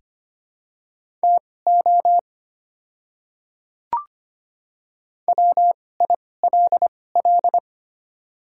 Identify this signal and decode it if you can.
Morse code, heard as a keyed tone, about 25 wpm, spaced as for 18 wpm, 700 Hz. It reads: TO WILL
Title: 2x Character Spacing - Top 100 Words - 25wpm